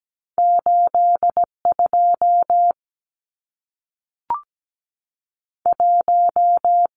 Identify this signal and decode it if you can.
Morse code, heard as a keyed tone, about 17 wpm, 700 Hz. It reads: 82 1
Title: Numbers - 2-Digit 17wpm